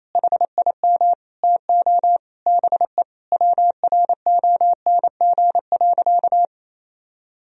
Morse code, heard as a keyed tone, about 28 wpm, 700 Hz. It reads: HIM TO BE WRONG.
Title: Sentences from Top 1000 Words - Review Part 3 - 28wpm